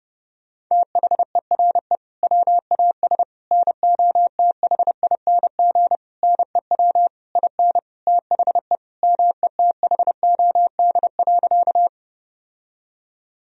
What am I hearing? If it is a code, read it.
Morse code, heard as a keyed tone, about 30 wpm, 700 Hz. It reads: THERE WAS NOTHING NEW IN THE METHOD.
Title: Sentences from Top 1000 Words - Review Part 3 - 30wpm